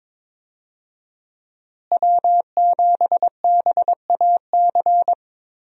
Morse code, heard as a keyed tone, about 22 wpm, 700 Hz. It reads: W7BAC
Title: USA Call Signs - 5 Characters - Repeated 3x - 22wpm